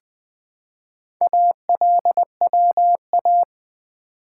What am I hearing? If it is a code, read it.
Morse code, heard as a keyed tone, about 20 wpm, 690 Hz. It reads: ALWA
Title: Build A Word - 2 Letters At A Time - 6-Letter Words - 20wpm